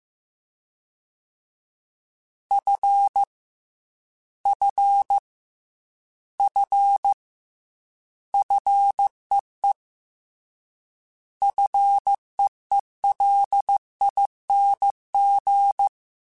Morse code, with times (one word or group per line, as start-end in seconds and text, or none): 2.5-3.2 F
4.5-5.2 F
6.4-7.1 F
8.3-9.7 FEE
11.4-15.9 FEELING